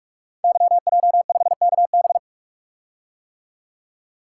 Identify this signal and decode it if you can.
Morse code, heard as a keyed tone, about 46 wpm, 700 Hz. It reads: YJ5KB
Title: Code Groups - 5 Letter:Numbers - 45wpm